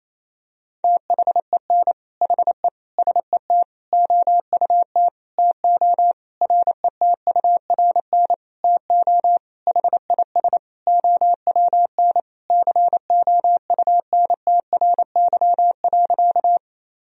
Morse code, heard as a keyed tone, about 28 wpm, 700 Hz. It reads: THEN HE SET OUT TO RETURN TO HIS OWN COUNTRY.
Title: Sentences from Top 1000 Words - Review Part 1 - 28wpm